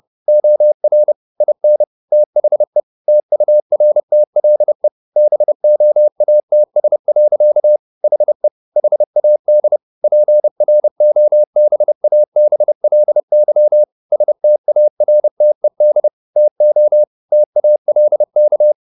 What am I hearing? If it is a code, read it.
Morse code, heard as a keyed tone, about 30 wpm, 600 Hz. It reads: OR IN THE TURTLE BOATS. HE HAD PROBABLY STARTED TO TALK
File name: the old man and the sea / the old man and the sea 1 30wpm